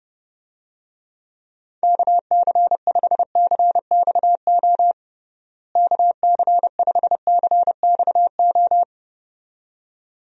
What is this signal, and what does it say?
Morse code, heard as a keyed tone, about 32 wpm, 700 Hz. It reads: KC5CXO KC5CXO
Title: USA Call Signs - 6 Characters - Repeated 2x - 30wpm